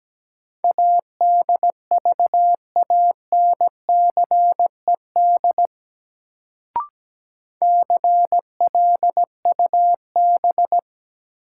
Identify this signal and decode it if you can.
Morse code, heard as a keyed tone, about 17 wpm, 700 Hz. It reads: ADVANCED CLUB
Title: Sentences from Top 1200 Words - New Word Drill Alt - 17wpm